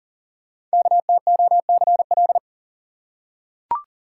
Morse code, heard as a keyed tone, about 39 wpm, 690 Hz. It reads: KTOCL
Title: Code Groups - 5 Letters - 40wpm